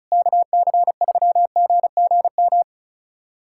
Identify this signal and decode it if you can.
Morse code, heard as a keyed tone, about 36 wpm, 700 Hz. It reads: KC3GGM
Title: USA Callsign DE Callsign - 6 Characters - 35wpm